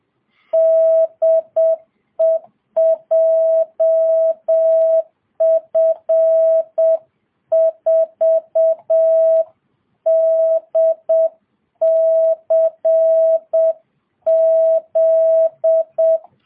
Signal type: Morse code